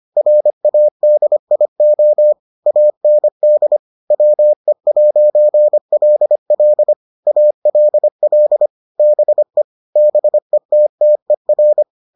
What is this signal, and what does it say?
Morse code, heard as a keyed tone, about 25 wpm, 600 Hz. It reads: RADIO AND WE'LL ALL BE BETTER